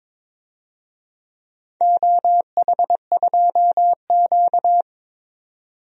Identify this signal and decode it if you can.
Morse code, heard as a keyed tone, about 23 wpm, 700 Hz. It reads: OH2Q
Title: DXCC-100 Call Signs - 4 Characters - 22wpm